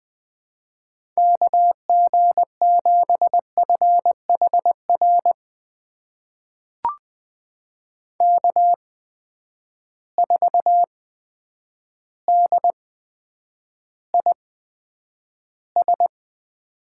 Morse code, heard as a keyed tone, about 20 wpm, spaced as for 9 wpm, 700 Hz. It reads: KG7FHR K4DIS
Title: USA Call Signs - 8x Character Spacing - 20wpm